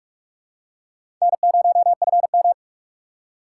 Morse code, heard as a keyed tone, about 44 wpm, 700 Hz. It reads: N0FM